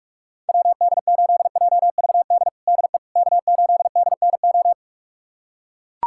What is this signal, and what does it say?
Morse code, heard as a keyed tone, about 42 wpm, 690 Hz. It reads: WD8JVD DE K8DNO
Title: USA Callsign DE Callsign 45wpm